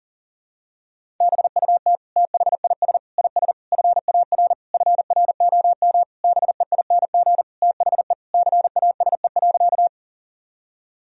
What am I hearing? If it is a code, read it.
Morse code, heard as a keyed tone, about 40 wpm, 700 Hz. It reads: BUT THIS IS FAR FROM BEING THE CASE.
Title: Sentences from Top 1000 Words - Review Part 3 - 40wpm